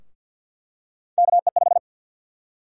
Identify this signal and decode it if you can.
Morse code, heard as a keyed tone, about 50 wpm, 700 Hz. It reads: KE5
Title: USA Call Sign Prefixes - Repeated 2x - 50wpm